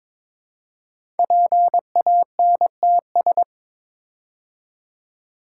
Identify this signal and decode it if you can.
Morse code, heard as a keyed tone, about 22 wpm, 700 Hz. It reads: PANTS
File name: 5 Letter Words - First 3 Letters Given - 22wpm